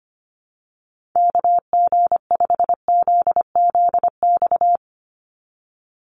Morse code, heard as a keyed tone, about 25 wpm, 700 Hz. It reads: KG5ZZX
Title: USA Call Signs 25wpm